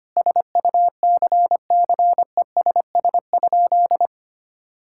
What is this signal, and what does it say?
Morse code, heard as a keyed tone, about 25 wpm, 700 Hz. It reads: SUCCESS?
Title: Sentences from Top 1000 Words - Review Part 6 - 25wpm